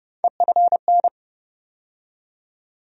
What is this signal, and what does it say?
Morse code, heard as a keyed tone, about 30 wpm, 700 Hz. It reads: EFN
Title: USA Call Sign Postfixes - 3 Letters 30wpm